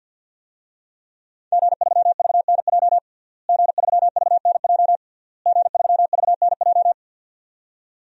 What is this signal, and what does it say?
Morse code, heard as a keyed tone, about 51 wpm, 700 Hz. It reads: G3VNJ G3VNJ G3VNJ
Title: DXCC-100 Call Signs - 5 Characters - Repeated 3x - 50wpm